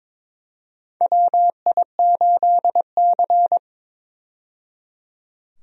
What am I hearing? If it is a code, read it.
Morse code, heard as a keyed tone, about 22 wpm, 700 Hz. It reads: WI8C
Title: USA Call Signs - 4 Characters - 22wpm